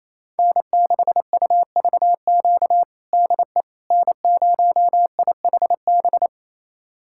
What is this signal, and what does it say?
Morse code, heard as a keyed tone, about 28 wpm, 700 Hz. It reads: N6UVQ DE N0IHB
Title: USA Callsign DE Callsign - 5 Characters - 28wpm